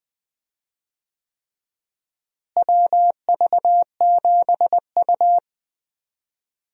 Morse code, heard as a keyed tone, about 20 wpm, 700 Hz. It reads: WV7U
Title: USA Call Signs - 4 Characters - Repeated 3x - 20wpm